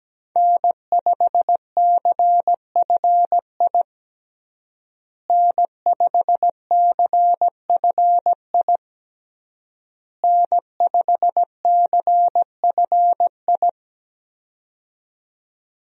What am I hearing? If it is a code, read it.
Morse code, heard as a keyed tone, about 17 wpm, 700 Hz. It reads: N5CFI N5CFI N5CFI
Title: USA Call Signs - 5 Characters - Repeated 3x - 17wpm